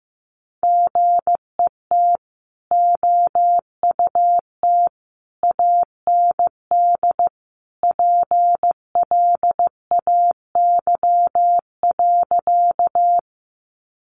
Morse code, heard as a keyed tone, about 15 wpm, 700 Hz. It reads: GET OUT AND PLAY.
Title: Sentences from Top 200 Words 15wpm